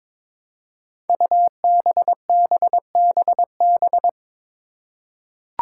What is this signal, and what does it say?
Morse code, heard as a keyed tone, about 22 wpm, 700 Hz. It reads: UBBBB
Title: Five Challenging Letter-Number Combinations 22wpm